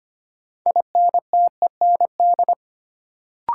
Morse code, heard as a keyed tone, about 25 wpm, 700 Hz. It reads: INTEND
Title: Sentences from Top 1600 Words - New Word Drill Alt - 25wpm